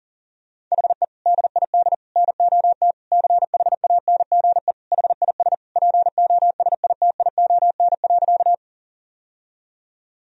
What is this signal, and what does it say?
Morse code, heard as a keyed tone, about 40 wpm, 700 Hz. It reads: HE DID NOT CHANGE HIS POSITION.
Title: Sentences from Top 1000 Words - Review Part 1 - 40wpm